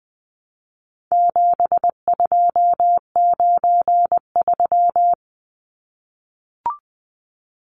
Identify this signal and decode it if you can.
Morse code, heard as a keyed tone, about 20 wpm, 700 Hz. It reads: 7293
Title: Numbers - 4-Digit 20wpm